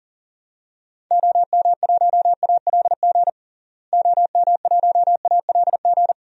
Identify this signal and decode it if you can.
Morse code, heard as a keyed tone, about 40 wpm, 700 Hz. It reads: OM1ALG OM1ALG
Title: DXCC-100 Call Signs - 6 Characters - Repeated 2x - 40wpm